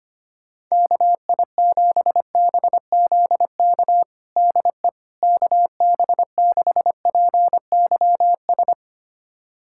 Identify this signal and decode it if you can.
Morse code, heard as a keyed tone, about 25 wpm, 700 Hz. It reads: KI7BZK DE KB6PYS
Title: USA Callsign DE Callsign - 6 Characters - 25wpm